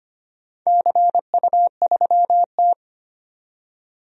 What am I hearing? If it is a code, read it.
Morse code, heard as a keyed tone, about 25 wpm, 700 Hz. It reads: CU3T
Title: DXCC-100 Call Signs - 4 Characters - 25wpm